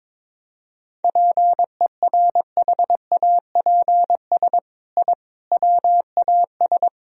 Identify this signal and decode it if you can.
Morse code, heard as a keyed tone, about 22 wpm, 700 Hz. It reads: PERHAPS I WAS
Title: Sentences from Top 1000 Words - Review Part 1 - 22wpm